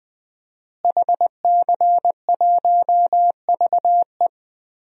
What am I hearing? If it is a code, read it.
Morse code, heard as a keyed tone, about 20 wpm, 700 Hz. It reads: HC1VE
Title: DXCC-100 Call Signs - 5 Characters - Repeated 2x - 20wpm